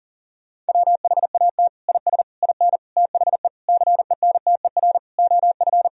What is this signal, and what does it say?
Morse code, heard as a keyed tone, about 40 wpm, 700 Hz. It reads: WHAT IS IN THE CENTER OF
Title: Sentences from Top 1000 Words - Review Part 2 - 40wpm